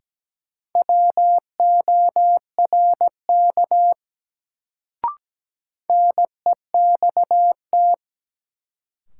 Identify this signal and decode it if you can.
Morse code, heard as a keyed tone, about 17 wpm, 700 Hz. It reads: WORK NEXT
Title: Sets of 2 Words - New Word Drill - 17wpm